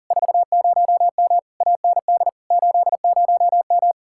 Morse code, heard as a keyed tone, about 48 wpm, 700 Hz. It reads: 40M AND 80M
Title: Speed-Racing Common QSO Phrases - 40wpm